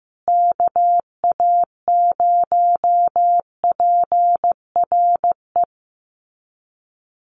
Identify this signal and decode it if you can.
Morse code, heard as a keyed tone, about 15 wpm, 700 Hz. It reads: KA0PRE
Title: USA Call Signs Repeated 3x 15wpm